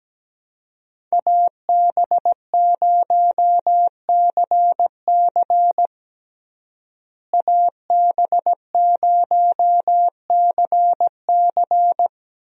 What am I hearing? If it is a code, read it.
Morse code, heard as a keyed tone, about 17 wpm, 700 Hz. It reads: AB0CC AB0CC
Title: USA Call Signs - 5 Characters - Repeated 2x - 17wpm